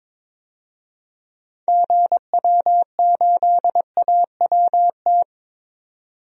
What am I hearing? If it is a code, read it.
Morse code, heard as a keyed tone, about 22 wpm, 700 Hz. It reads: GW8AWT
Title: DXCC-100 Call Signs - 6 Characters - 22wpm